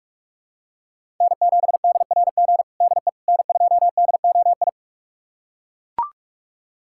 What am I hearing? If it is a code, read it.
Morse code, heard as a keyed tone, about 41 wpm, 700 Hz. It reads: N7DRG DE N2DOI
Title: USA Callsign DE Callsign - 5 Characters - 45wpm